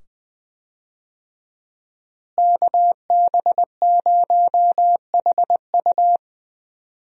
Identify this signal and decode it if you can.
Morse code, heard as a keyed tone, about 20 wpm, 700 Hz. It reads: KB0HU
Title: USA Call Signs - 5 Characters - Repeated 2x - 20wpm